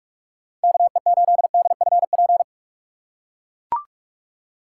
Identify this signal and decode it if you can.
Morse code, heard as a keyed tone, about 41 wpm, 700 Hz. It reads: KE8DFP
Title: USA Call Signs - 6x Character Spacing - 45wpm